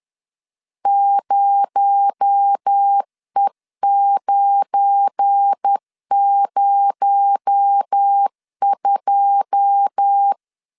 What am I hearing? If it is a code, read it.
Morse code, heard as a keyed tone, about 10 wpm, 800 Hz. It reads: 0E902
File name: CW / DET / 1 9 1500C S A 10wpm 4511